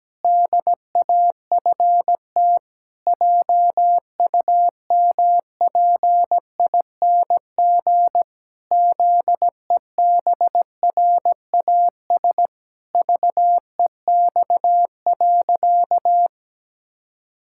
Morse code, heard as a keyed tone, about 17 wpm, 700 Hz. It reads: DAFT JUMPING ZEBRAS VEX.
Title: Pangram Challenge v2 - 17wpm